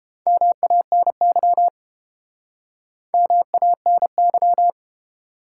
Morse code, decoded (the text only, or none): MANY MANY